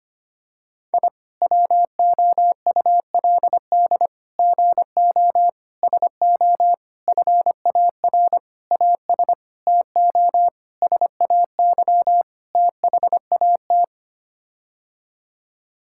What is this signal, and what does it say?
Morse code, heard as a keyed tone, about 25 wpm, 700 Hz. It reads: I WOULD GO SO FAR AS TO SAY THAT
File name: Sets of 9 Words - 25wpm